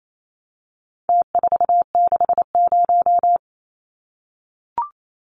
Morse code, heard as a keyed tone, about 28 wpm, 700 Hz. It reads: T460
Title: Numbers - Contest Style - 4-Digit 28wpm